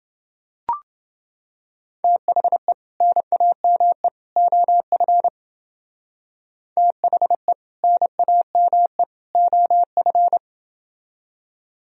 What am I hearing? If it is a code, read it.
Morse code, heard as a keyed tone, about 30 wpm, 700 Hz. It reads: THE NAME OF THE NAME OF